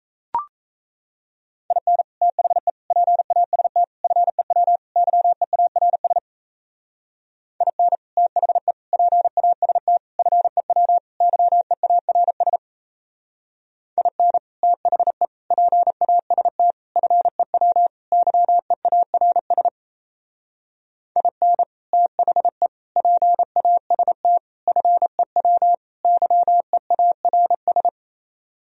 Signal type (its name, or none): Morse code